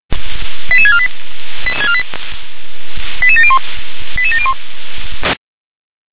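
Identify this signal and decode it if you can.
Five-tone selective call (ZVEI), heard as quick runs of five tones, tones of about 70 ms. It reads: 88648 88701 88701
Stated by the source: ZVEI five-tone selective call received via SDR.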